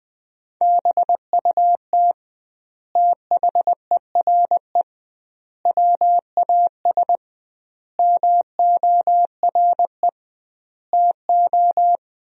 Morse code, heard as a keyed tone, about 21 wpm, 700 Hz. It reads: BUT THERE WAS MORE TO
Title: 2x Word-Spacing - Sentences from Top 100 Words - 20wpm